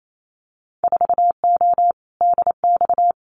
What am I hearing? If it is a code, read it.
Morse code, heard as a keyed tone, about 28 wpm, 700 Hz. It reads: 4O DX
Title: Winter Field Day Exchanges Encore 28wpm